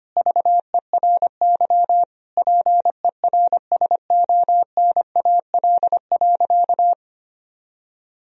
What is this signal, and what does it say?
Morse code, heard as a keyed tone, about 25 wpm, 700 Hz. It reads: VERY PERSONAL.
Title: Sentences from Top 1000 Words - Review Part 4 - 25wpm